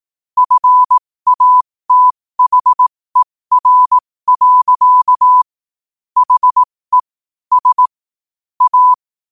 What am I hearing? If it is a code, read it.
Morse code, heard as a keyed tone, about 18 wpm, spaced as for 15 wpm, 1000 Hz. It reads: FATHER. HE S A
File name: qotdfiles / 15wpm